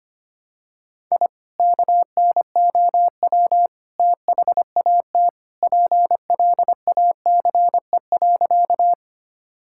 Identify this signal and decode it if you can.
Morse code, heard as a keyed tone, about 25 wpm, 700 Hz. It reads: I KNOW THAT PLACE.
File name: practice / 25wpm / Sentences with 2-Word Phrases - 25wpm